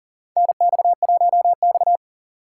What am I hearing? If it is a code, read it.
Morse code, heard as a keyed tone, about 38 wpm, 700 Hz. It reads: NX1X
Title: USA Call Signs - 4x Character Spacing - 40wpm